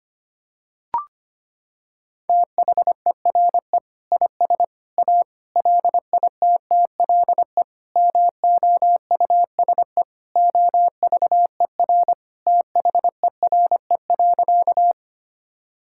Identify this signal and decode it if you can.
Morse code, heard as a keyed tone, about 25 wpm, 700 Hz. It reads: THERE IS A LITTLE MOUSE OVER THERE.
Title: Sentences from Top 1000 Words - Review Part 5 - 25wpm